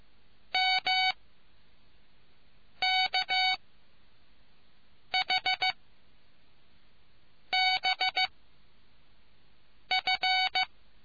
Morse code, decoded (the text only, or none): MKHBF